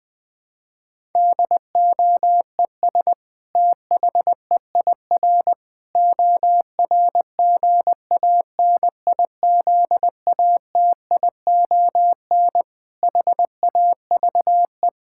Morse code, heard as a keyed tone, about 21 wpm, 700 Hz. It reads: DOES THEIR ORGANIZATION HAVE
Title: Sentences from Top 1000 Words - Review Part 5 - 20wpm